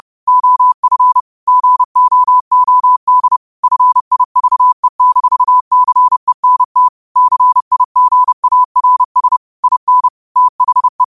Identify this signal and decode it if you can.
Morse code, heard as a keyed tone, about 30 wpm, 1000 Hz. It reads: OF GOOD FIVE-CENT CIGARS IN THE
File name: qotdfiles / 30wpm